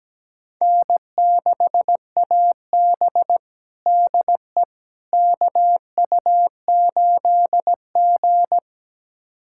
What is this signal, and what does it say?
Morse code, heard as a keyed tone, about 17 wpm, 700 Hz. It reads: N6AB DE KU8G
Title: USA Callsign DE Callsign - 4 Characters - 17wpm